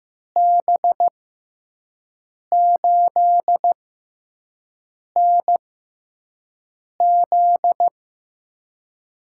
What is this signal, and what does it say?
Morse code, heard as a keyed tone, about 15 wpm, spaced as for 5 wpm, 700 Hz. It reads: B8NZ
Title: USA Call Signs - 6 Characters - 6x Character Spacing - 15wpm